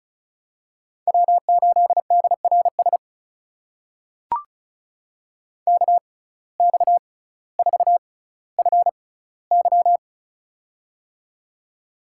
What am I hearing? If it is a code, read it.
Morse code, heard as a keyed tone, about 34 wpm, spaced as for 18 wpm, 700 Hz. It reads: W8DRS KX4FY
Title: USA Call Signs - 5 Characters - 6x Character Spacing - 35wpm